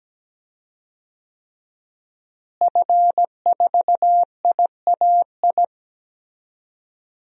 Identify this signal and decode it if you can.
Morse code, heard as a keyed tone, about 17 wpm, 700 Hz. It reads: F4IAI
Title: DXCC-100 Call Signs - 5 Characters - Repeated 3x - 17wpm